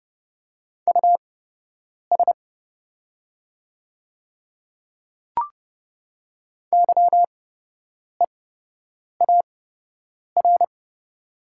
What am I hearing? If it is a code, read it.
Morse code, heard as a keyed tone, about 30 wpm, spaced as for 8 wpm, 700 Hz. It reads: US YEAR
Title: 8x Character Spacing - Top 200 Words - 30wpm